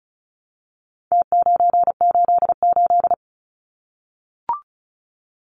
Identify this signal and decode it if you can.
Morse code, heard as a keyed tone, about 34 wpm, 700 Hz. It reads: T988
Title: Numbers - Contest Style - 4-Digit 35wpm